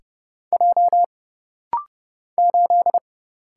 Morse code, heard as a keyed tone, about 30 wpm, 700 Hz. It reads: J 8
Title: Single Letter-Number - Rapid-Fire 30wpm